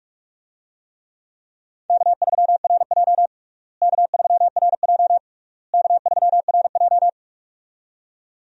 Morse code, heard as a keyed tone, about 46 wpm, 700 Hz. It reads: K3RJ K3RJ K3RJ
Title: USA Call Signs - 4 Characters - Repeated 3x - 45wpm